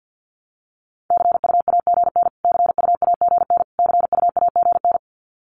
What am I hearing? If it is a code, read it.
Morse code, heard as a keyed tone, about 50 wpm, 700 Hz. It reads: CUAGN CUAGN CUAGN
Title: Abbreviations Repeated 3x - 50wpm